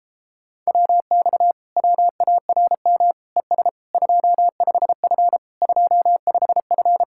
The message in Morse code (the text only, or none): WX WARM ES 25F 25F